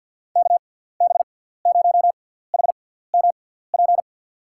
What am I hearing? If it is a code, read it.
Morse code, heard as a keyed tone, about 52 wpm, spaced as for 17 wpm, 690 Hz. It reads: KB0HMP
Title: USA Call Signs - 6x Character Spacing - 50wpm